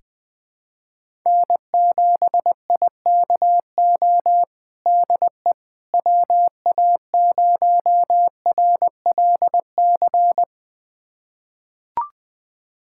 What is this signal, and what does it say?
Morse code, heard as a keyed tone, about 20 wpm, 700 Hz. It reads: N7IKO DE WA0RLC
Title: USA Callsign DE Callsign 20wpm